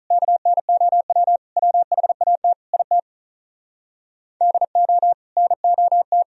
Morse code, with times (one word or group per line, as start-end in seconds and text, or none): 0.1-1.4 KNOW
1.6-2.5 WHAT
2.7-3.0 IT
4.4-5.1 DO
5.4-6.2 NOT